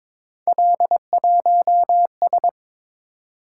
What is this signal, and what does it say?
Morse code, heard as a keyed tone, about 22 wpm, 700 Hz. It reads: L1S
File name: practice / 22wpm / Three Letter-Number Combinations - 22wpm